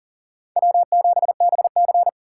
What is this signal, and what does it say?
Morse code, heard as a keyed tone, about 40 wpm, 700 Hz. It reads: W7BC
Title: USA Call Signs - 4x Character Spacing - 40wpm